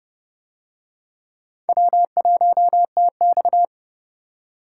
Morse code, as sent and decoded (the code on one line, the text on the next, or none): .-- .---- - -..-
W1TX